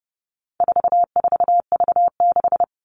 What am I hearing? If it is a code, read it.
Morse code, heard as a keyed tone, about 30 wpm, 700 Hz. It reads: 44V6
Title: All Cut Numbers - 4-Digit 30wpm